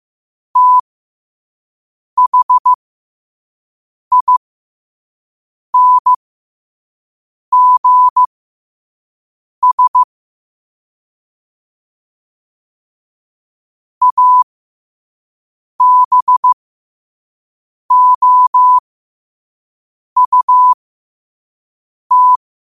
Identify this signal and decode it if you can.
Morse code, heard as a keyed tone, about 15 wpm, spaced as for 5 wpm, 1000 Hz. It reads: THINGS ABOUT
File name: qotdfiles / 5wpm